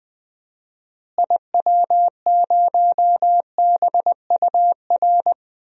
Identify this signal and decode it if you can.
Morse code, heard as a keyed tone, about 21 wpm, 700 Hz. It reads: IW0BUR
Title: DXCC-100 Call Signs - 6 Characters - Repeated 3x - 20wpm